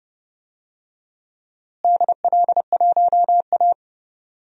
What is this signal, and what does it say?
Morse code, heard as a keyed tone, about 30 wpm, 700 Hz. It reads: DL1A
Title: DXCC-100 Call Signs - 4 Characters - Repeated 3x - 30wpm